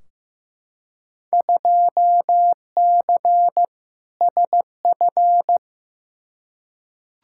Morse code, heard as a keyed tone, about 15 wpm, 700 Hz. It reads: 2C SF
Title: Field Day Exchanges Encore - 15wpm